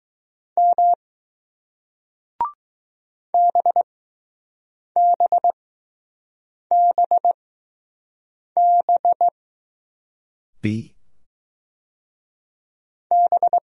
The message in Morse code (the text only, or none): M BBBB B